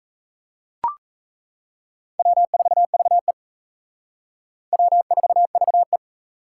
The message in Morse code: W4VE W4VE

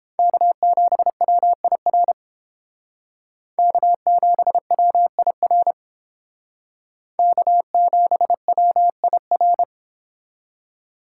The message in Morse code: K7WIR K7WIR K7WIR

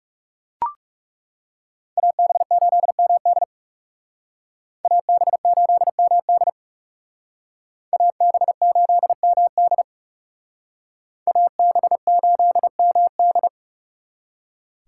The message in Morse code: AB8MD AB8MD AB8MD AB8MD

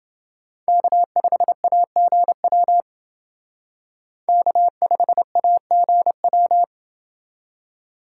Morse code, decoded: K5AGW K5AGW

